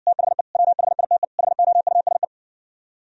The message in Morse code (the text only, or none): THE WHITE HOUSE